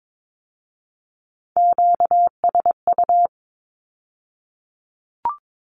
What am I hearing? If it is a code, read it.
Morse code, heard as a keyed tone, about 22 wpm, 700 Hz. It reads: QSU
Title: Q-Codes with Spoken Meaning - 22wpm